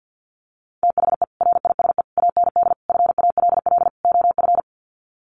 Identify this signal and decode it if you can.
Morse code, heard as a keyed tone, about 50 wpm, 700 Hz. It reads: THE RISE AND FALL OF
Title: Sets of 5 Words Encore2 50wpm